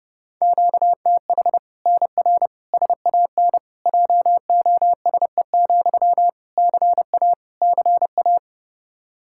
Morse code, heard as a keyed tone, about 30 wpm, 700 Hz. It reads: QTH NR SAN JOSE, CA CA
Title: Practice QSO - Exchange 1 - Beginning - 30wpm